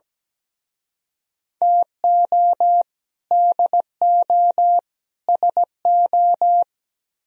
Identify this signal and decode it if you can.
Morse code, heard as a keyed tone, about 20 wpm, 700 Hz. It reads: TO DO SO
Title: Speed-Racing - Sets of 3 Words 17wpm